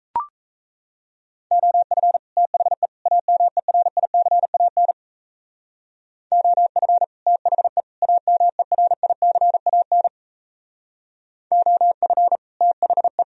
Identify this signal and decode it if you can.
Morse code, heard as a keyed tone, about 35 wpm, 700 Hz. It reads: OF THE AMERICAN OF THE AMERICAN OF THE
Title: Speed-Racing - Sets of 3 Words 28wpm